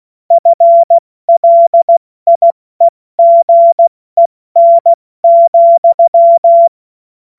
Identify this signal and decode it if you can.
Morse code, heard as a keyed tone, about 16 wpm, 680 Hz. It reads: FLIEGEN,
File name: Morsekurs / Witze / Witz 16 Januar 2024 70 F80